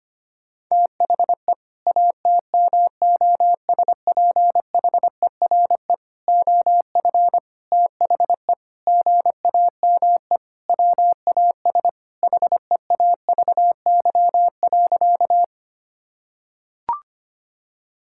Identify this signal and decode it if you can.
Morse code, heard as a keyed tone, about 25 wpm, 700 Hz. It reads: THE ATMOSPHERE OF THE GAME WAS HEAVY.
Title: Sentences from Top 1600 Words 25wpm